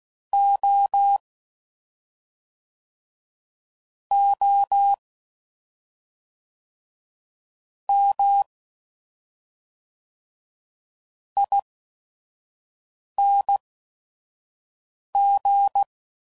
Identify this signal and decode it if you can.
Morse code, heard as a keyed tone, about 16 wpm, spaced as for 5 wpm, 800 Hz. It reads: O O M ING
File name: Suffixes / Special ING's / SlowToFaster ING's